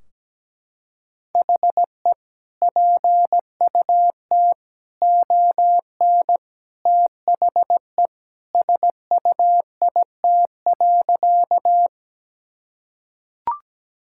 Morse code, heard as a keyed tone, about 17 wpm, 700 Hz. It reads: HE PUT ON THE SUIT.
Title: Sentences from Top 1600 Words 17wpm